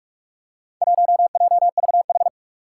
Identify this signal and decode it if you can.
Morse code, heard as a keyed tone, about 47 wpm, 700 Hz. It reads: 1JVH
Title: Four Challenging Letter-Number Combinations 45wpm